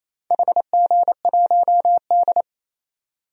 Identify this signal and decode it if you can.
Morse code, heard as a keyed tone, about 28 wpm, 690 Hz. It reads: HG1D
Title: DXCC-100 Call Signs - 4 Characters - 28wpm